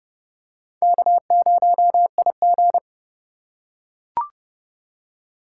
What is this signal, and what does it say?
Morse code, heard as a keyed tone, about 30 wpm, 700 Hz. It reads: K0IG